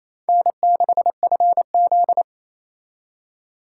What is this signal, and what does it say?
Morse code, heard as a keyed tone, about 35 wpm, 700 Hz. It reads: N6FZ